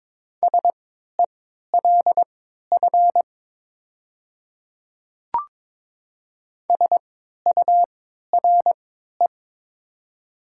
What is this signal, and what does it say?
Morse code, heard as a keyed tone, about 23 wpm, spaced as for 13 wpm, 700 Hz. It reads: SELF SURE
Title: 3x Character Spacing - Top 500 Words - 22wpm